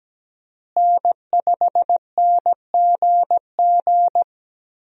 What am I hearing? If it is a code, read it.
Morse code, heard as a keyed tone, about 17 wpm, 700 Hz. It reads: N5NGG
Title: USA Call Signs - 5 Characters - Repeated 3x - 17wpm